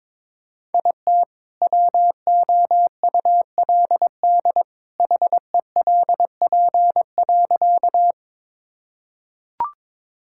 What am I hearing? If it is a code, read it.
Morse code, heard as a keyed tone, about 21 wpm, 700 Hz. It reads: IT WOULD HELP.
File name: practice / 22wpm / Sentences with 2-Word Phrases - 22wpm